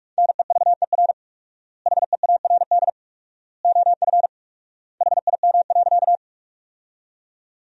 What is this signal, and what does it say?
Morse code, heard as a keyed tone, about 41 wpm, 700 Hz. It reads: NEVER HEARD OF HIM.
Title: 4x Word-Spacing - Sentences from Top 400 Words - 45wpm